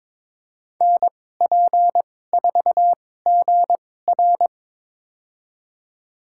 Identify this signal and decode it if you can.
Morse code, heard as a keyed tone, about 22 wpm, spaced as for 16 wpm, 700 Hz. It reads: NP4GR
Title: USA Call Signs - 5 Characters - 2x Character Spacing - 22wpm